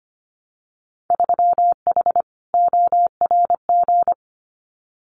Morse code, heard as a keyed tone, about 25 wpm, 700 Hz. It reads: 3H ORG
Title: Winter Field Day Exchanges Encore 25wpm